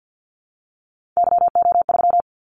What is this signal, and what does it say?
Morse code, heard as a keyed tone, about 50 wpm, 700 Hz. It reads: YO3
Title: Callsign Prefixes for Top 25 DXCC - 50wpm